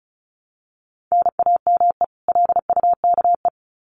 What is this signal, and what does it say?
Morse code, heard as a keyed tone, about 34 wpm, 700 Hz. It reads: NAME LUKE
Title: Common QSO Phrases 35wpm